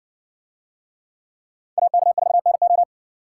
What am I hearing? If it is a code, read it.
Morse code, heard as a keyed tone, about 57 wpm, 700 Hz. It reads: AK4NO